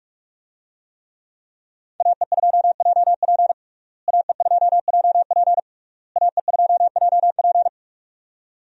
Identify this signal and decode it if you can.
Morse code, heard as a keyed tone, about 43 wpm, 700 Hz. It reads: AE2JP AE2JP AE2JP
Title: USA Call Signs - 5 Characters - Repeated 3x - 45wpm